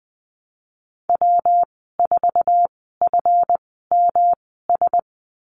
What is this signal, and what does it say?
Morse code, heard as a keyed tone, about 20 wpm, spaced as for 15 wpm, 700 Hz. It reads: W4FMS
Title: 140 Lesson 23g - Letter f - Call Signs 20wpm